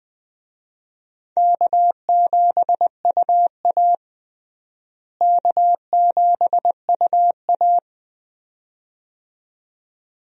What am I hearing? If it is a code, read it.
Morse code, heard as a keyed tone, about 20 wpm, 700 Hz. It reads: K7UA K7UA
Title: USA Call Signs - 4 Characters - Repeated 2x - 20wpm